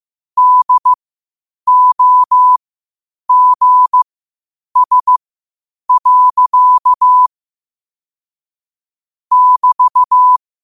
Morse code, heard as a keyed tone, about 15 wpm, spaced as for 8 wpm, 1000 Hz. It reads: DOGS. <BT>
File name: qotdfiles / 8wpm